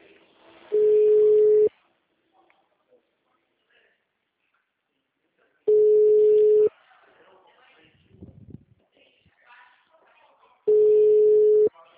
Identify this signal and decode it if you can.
Ringback tone (Europe, 425 Hz), with speech.
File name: out- -2003- - - .78823